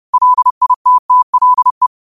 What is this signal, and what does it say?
Morse code, heard as a keyed tone, about 30 wpm, 1000 Hz. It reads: LITTLE
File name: qotdfiles / 30wpm